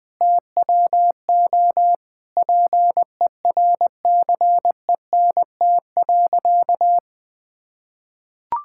Morse code, heard as a keyed tone, about 20 wpm, 700 Hz. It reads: TWO PERCENT.
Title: Sentences with 8-Word Phrases - 20wpm